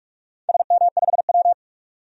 Morse code, heard as a keyed tone, about 47 wpm, 700 Hz. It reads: SM5W
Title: DXCC-100 Call Signs - 4 Characters - 45wpm